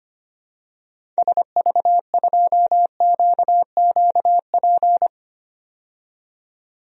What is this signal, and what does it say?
Morse code, heard as a keyed tone, about 25 wpm, 700 Hz. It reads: SV2QQP
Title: DXCC-100 Call Signs - 6 Characters - 25wpm